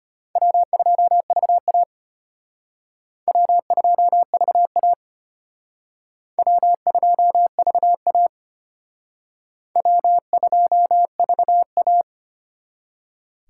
Morse code, heard as a keyed tone, about 34 wpm, 700 Hz. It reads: W2VA W2VA W2VA W2VA